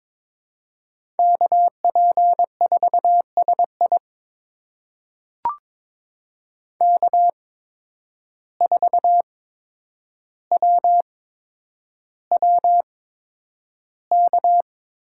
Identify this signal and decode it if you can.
Morse code, heard as a keyed tone, about 22 wpm, spaced as for 9 wpm, 700 Hz. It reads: KP4SI K4WWK
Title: USA Call Signs - 8x Character Spacing - 22wpm